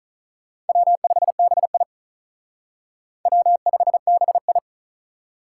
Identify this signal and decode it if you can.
Morse code, heard as a keyed tone, about 43 wpm, 700 Hz. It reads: W5BI W5BI
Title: Speed Racing - USA Call Signs 35wpm